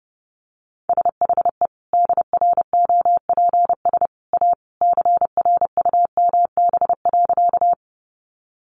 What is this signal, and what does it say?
Morse code, heard as a keyed tone, about 30 wpm, 700 Hz. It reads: SHE DROPS A CRUMB.
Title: Additional Easy Phrases 30wpm